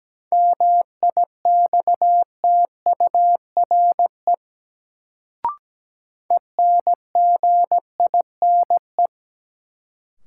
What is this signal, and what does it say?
Morse code, heard as a keyed tone, about 17 wpm, 700 Hz. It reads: MIXTURE ENGINE
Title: Sentences from Top 1500 Words - New Word Drill - 17wpm